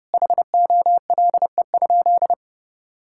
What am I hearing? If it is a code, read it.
Morse code, heard as a keyed tone, about 30 wpm, 700 Hz. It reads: HOLE?
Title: Sentences from Top 1600 Words 30wpm